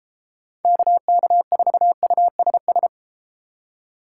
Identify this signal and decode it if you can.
Morse code, heard as a keyed tone, about 31 wpm, 700 Hz. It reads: KK4USS